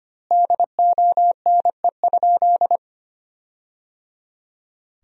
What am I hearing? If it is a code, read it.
Morse code, heard as a keyed tone, about 25 wpm, 700 Hz. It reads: DONE?
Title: 2x Word-Spacing - Sentences from Top 500 Words - 25wpm